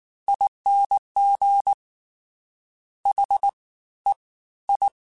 Morse code, heard as a keyed tone, about 19 wpm, spaced as for 14 wpm, 800 Hz. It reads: ING HEI